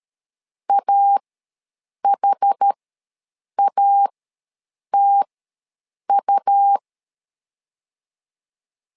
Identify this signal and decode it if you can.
Morse code, heard as a keyed tone, about 13 wpm, spaced as for 7 wpm, 800 Hz. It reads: AHATU
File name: LEZIONI / 2 UHDN / 2 UHD ws 12wpm 1722